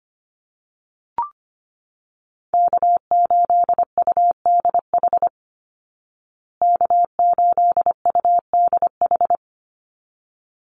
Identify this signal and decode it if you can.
Morse code, heard as a keyed tone, about 25 wpm, 700 Hz. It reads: K8UDH K8UDH
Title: Call Signs Repeated 3x - 25wpm